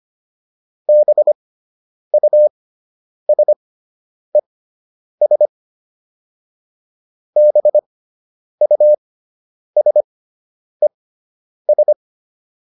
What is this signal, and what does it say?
Morse code, heard as a keyed tone, about 25 wpm, spaced as for 9 wpm, 600 Hz. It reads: BUSES BUSES